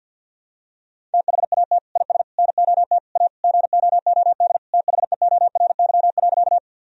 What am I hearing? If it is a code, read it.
Morse code, heard as a keyed tone, about 50 wpm, 700 Hz. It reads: THAT IS NOT A GOOD THEORY.
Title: Sentences from Top 1000 Words - Review Part 3 - 50wpm